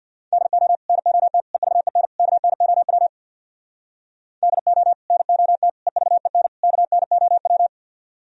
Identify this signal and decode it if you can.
Morse code, heard as a keyed tone, about 60 wpm, 700 Hz. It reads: DO NOT EVEN KNOW DO NOT EVEN KNOW
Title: Speed-Racing - Sets of 4 Words 50wpm